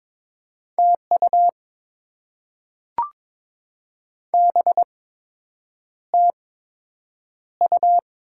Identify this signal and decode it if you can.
Morse code, heard as a keyed tone, about 22 wpm, spaced as for 8 wpm, 700 Hz. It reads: TU BTU
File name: Top 20 QSO Elements - 8x Character Spacing - 22wpm